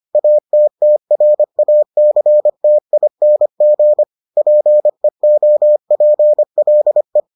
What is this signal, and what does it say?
Morse code, heard as a keyed tone, about 25 wpm, 600 Hz. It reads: ATTRACTING PEOPLE